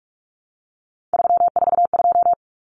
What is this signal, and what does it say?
Morse code, heard as a keyed tone, about 44 wpm, 700 Hz. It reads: 342
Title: Numbers - 3-Digit 45wpm